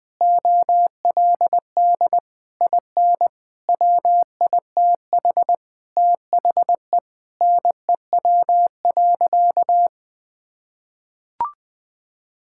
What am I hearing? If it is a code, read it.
Morse code, heard as a keyed tone, about 20 wpm, 700 Hz. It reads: OLD IN WITH THE NEW.
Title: Sentences with 8-Word Phrases - 20wpm